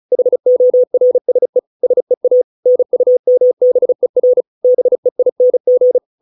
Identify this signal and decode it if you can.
Morse code, heard as a keyed tone, about 34 wpm, 500 Hz. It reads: HORSE SEA NUMBER BEING
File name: 632 words / 632 Words 35WPM